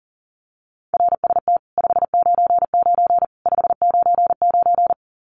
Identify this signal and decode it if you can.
Morse code, heard as a keyed tone, about 40 wpm, 700 Hz. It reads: RST 599 599
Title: Common QSO Phrases 40wpm